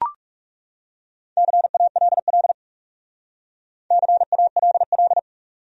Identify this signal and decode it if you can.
Morse code, heard as a keyed tone, about 42 wpm, 700 Hz. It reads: CALL CALL